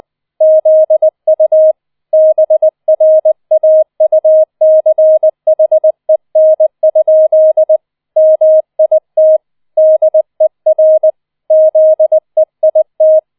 Morse code, heard as a keyed tone, about 20 wpm, 630 Hz. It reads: ZU BRAUCHEN? MIT DER ZEIT